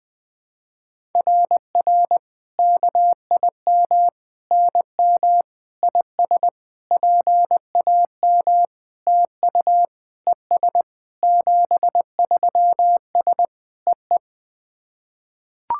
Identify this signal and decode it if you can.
Morse code, heard as a keyed tone, about 20 wpm, 700 Hz. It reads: RR KIM NM IS PAM TU ES 73S EE
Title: SOTA QSO Practice 20wpm